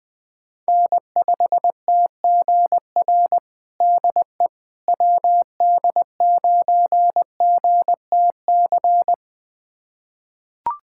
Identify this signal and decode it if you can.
Morse code, heard as a keyed tone, about 20 wpm, 700 Hz. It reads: N5TGR DE WD9GTC
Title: USA Callsign DE Callsign 20wpm